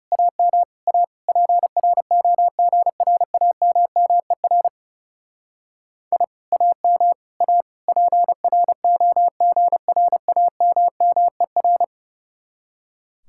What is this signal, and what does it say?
Morse code, heard as a keyed tone, about 35 wpm, 700 Hz. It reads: AM A PROGRAMMER I AM A PROGRAMMER
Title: Speed-Racing Common QSO Phrases - 30wpm